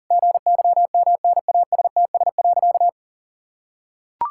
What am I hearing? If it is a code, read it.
Morse code, heard as a keyed tone, about 37 wpm, 700 Hz. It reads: GYMNASTS.
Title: Pangram Challenge v2 - 40wpm